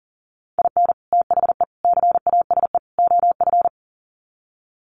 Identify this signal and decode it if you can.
Morse code, heard as a keyed tone, about 40 wpm, 700 Hz. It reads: IN THE CASE OF
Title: Sets of 4 Words 40wpm